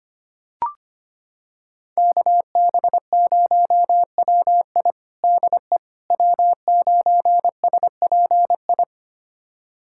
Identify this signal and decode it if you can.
Morse code, heard as a keyed tone, about 25 wpm, 700 Hz. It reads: KB0WI DE W9SPI
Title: USA Callsign DE Callsign - 5 Characters - 25wpm